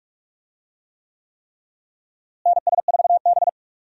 Morse code, heard as a keyed tone, about 42 wpm, 700 Hz. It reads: NS4B
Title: USA Call Signs - 4 Characters - Repeated 3x - 45wpm